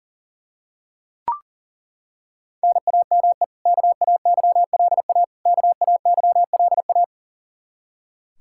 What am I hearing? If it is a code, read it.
Morse code, heard as a keyed tone, about 40 wpm, 700 Hz. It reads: NAME KAYLA KAYLA
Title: Practice QSO - Exchange 1 - Beginning - 40wpm